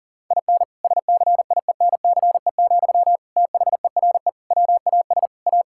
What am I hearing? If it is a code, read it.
Morse code, heard as a keyed tone, about 40 wpm, 700 Hz. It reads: IN SCIENCE, THERE WAS A
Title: Sentences from Top 1000 Words - Review Part 5 - 40wpm